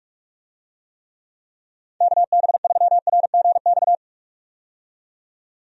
Morse code, heard as a keyed tone, about 44 wpm, 700 Hz. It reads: KB3RGX
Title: USA Call Signs - 6 Characters - 45wpm